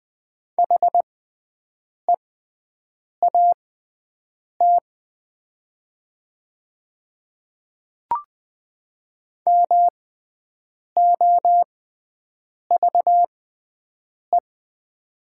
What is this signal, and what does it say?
Morse code, heard as a keyed tone, about 20 wpm, spaced as for 7 wpm, 700 Hz. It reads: HEAT MOVE
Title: 6x Character Spacing - Top 500 Words - 20wpm